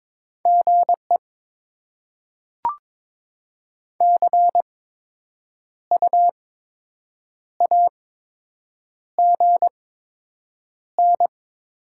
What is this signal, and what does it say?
Morse code, heard as a keyed tone, about 22 wpm, spaced as for 7 wpm, 700 Hz. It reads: GE CU AGN
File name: Top 20 QSO Elements - 8x Character Spacing - 22wpm